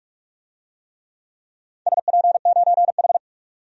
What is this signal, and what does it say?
Morse code, heard as a keyed tone, about 45 wpm, 700 Hz. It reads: SP9H
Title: DXCC-100 Call Signs - 4 Characters - 45wpm